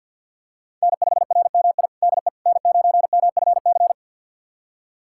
Morse code, heard as a keyed tone, about 51 wpm, 700 Hz. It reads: N5RMI DE N9MFC
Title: USA Callsign DE Callsign - 5 Characters - 50wpm